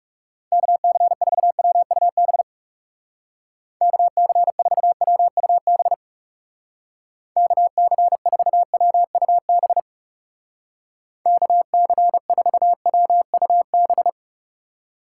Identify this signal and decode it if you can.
Morse code, heard as a keyed tone, about 38 wpm, 700 Hz. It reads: KC4WUB KC4WUB KC4WUB KC4WUB